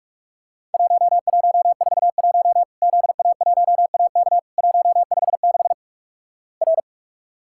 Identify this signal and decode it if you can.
Morse code, heard as a keyed tone, about 46 wpm, 700 Hz. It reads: 1141 ZA1AK 156 R
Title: Worked All Europe DX Contest - QTC Exchange - 45wpm